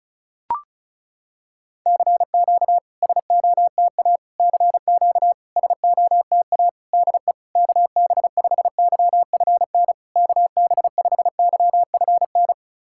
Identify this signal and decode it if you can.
Morse code, heard as a keyed tone, about 36 wpm, 690 Hz. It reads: CQ SOTA CQ SOTA DE KB5YFN KB5YFN
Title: SOTA QSO Practice 35wpm